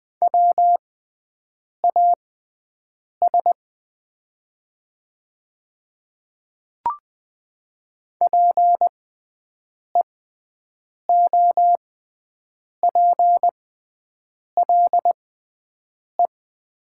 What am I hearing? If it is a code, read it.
Morse code, heard as a keyed tone, about 20 wpm, spaced as for 7 wpm, 700 Hz. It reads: WAS PEOPLE
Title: 6x Character Spacing - Top 100 Words - 20wpm